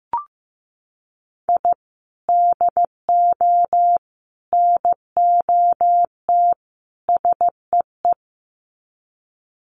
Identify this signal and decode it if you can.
Morse code, heard as a keyed tone, about 15 wpm, 700 Hz. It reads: I DO NOT SEE
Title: Sets of 4 Words 15wpm